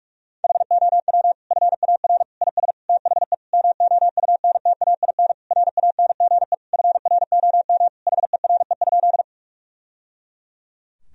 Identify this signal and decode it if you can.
Morse code, heard as a keyed tone, about 41 wpm, 700 Hz. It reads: HOW FAR IS THE MOUNTAIN RANGE FROM HERE?
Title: Sentences from Top 1000 Words - Review Part 4 - 45wpm